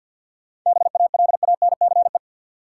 Morse code, heard as a keyed tone, about 53 wpm, 700 Hz. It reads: BALANCE
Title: Sentences from Top 900 Words - New Word Drill - 50wpm